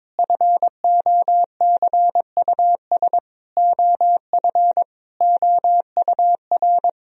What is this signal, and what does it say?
Morse code, heard as a keyed tone, about 22 wpm, 700 Hz. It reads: FOCUS OF OUR
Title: Sentences from Top 1000 Words - Review Part 4 - 22wpm